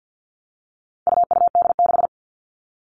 Morse code, heard as a keyed tone, about 50 wpm, 700 Hz. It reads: UUD6
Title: All Cut Numbers - 4-Digit 50wpm